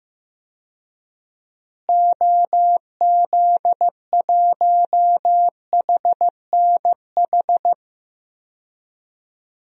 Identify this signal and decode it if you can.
Morse code, heard as a keyed tone, about 15 wpm, 700 Hz. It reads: OZ1HNH
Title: DXCC-100 Call Signs - 6 Characters - Repeated 3x - 15wpm